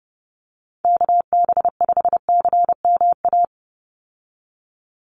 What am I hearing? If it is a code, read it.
Morse code, heard as a keyed tone, about 28 wpm, 700 Hz. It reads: KB5CMA
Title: USA Call Signs 30wpm